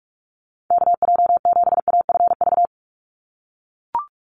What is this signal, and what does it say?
Morse code, heard as a keyed tone, about 44 wpm, 690 Hz. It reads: KJ7AFV
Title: USA Call Signs 45wpm